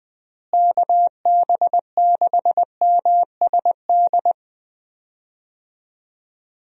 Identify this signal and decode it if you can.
Morse code, heard as a keyed tone, about 20 wpm, 700 Hz. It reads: KB6MSD
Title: USA Call Signs - 6 Characters - Repeated 3x - 20wpm